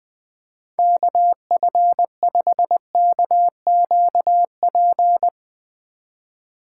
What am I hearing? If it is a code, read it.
Morse code, heard as a keyed tone, about 20 wpm, 700 Hz. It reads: KF5KQP
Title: USA Call Signs - 6 Characters - 2x Character Spacing - 20wpm